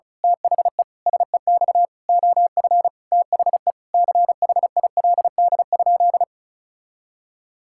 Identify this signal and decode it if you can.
Morse code, heard as a keyed tone, about 36 wpm, 700 Hz. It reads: THE SEX OF THE CHILD?
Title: Sentences from Top 1600 Words 35wpm